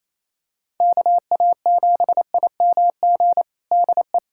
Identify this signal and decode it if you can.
Morse code, heard as a keyed tone, about 27 wpm, 700 Hz. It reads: KA7IMG DE
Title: USA Callsign DE Callsign - 6 Characters - 28wpm